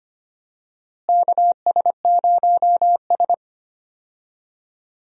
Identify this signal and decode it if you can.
Morse code, heard as a keyed tone, about 25 wpm, 700 Hz. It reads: KS0S
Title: USA Call Signs - 4 Characters - Repeated 2x - 25wpm